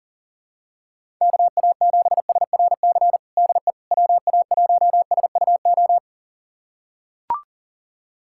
Morse code, heard as a keyed tone, about 40 wpm, 700 Hz. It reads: KA7SRC DE WA1SUO
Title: USA Callsign DE Callsign - 6 Characters - 40wpm